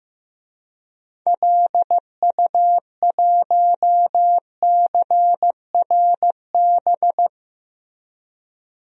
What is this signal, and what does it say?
Morse code, heard as a keyed tone, about 16 wpm, 700 Hz. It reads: LU1CRB
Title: DXCC-100 Call Signs - 6 Characters - Repeated 2x - 15wpm